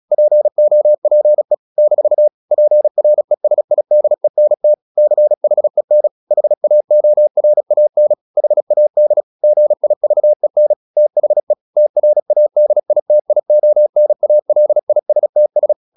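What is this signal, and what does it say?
Morse code, heard as a keyed tone, about 36 wpm, 600 Hz. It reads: POPE - PRESIDENT CHEN HAORAN HAD GIVEN THE TRADITIONALISTS
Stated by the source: Morse practice at 36 wpm, text of Ken Follett's novel Never, from CW med Gnister.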